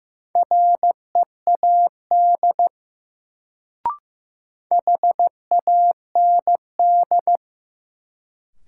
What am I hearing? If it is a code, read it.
Morse code, heard as a keyed tone, about 15 wpm, 700 Hz. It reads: READ HAND
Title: Sentences from Top 200 Words - New Word Drill - 15wpm